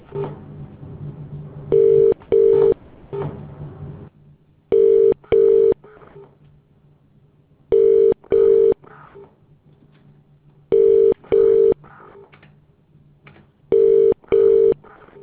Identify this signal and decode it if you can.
Telephone call progress tone (UK), heard as a telephone network's tone, about 400, 450 Hz. Ringback tone.